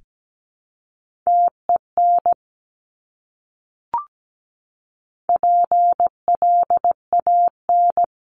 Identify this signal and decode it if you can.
Morse code, heard as a keyed tone, about 17 wpm, 700 Hz. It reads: TEN PLAN
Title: Top 500 Words Repeated 17wpm